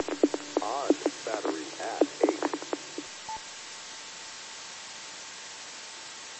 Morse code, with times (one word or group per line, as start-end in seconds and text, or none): none